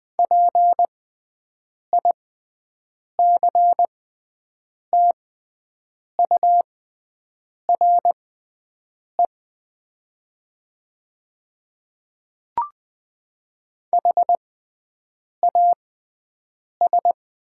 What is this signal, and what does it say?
Morse code, heard as a keyed tone, about 20 wpm, spaced as for 7 wpm, 700 Hz. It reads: PICTURE HAS
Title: 6x Character Spacing - Top 200 Words - 20wpm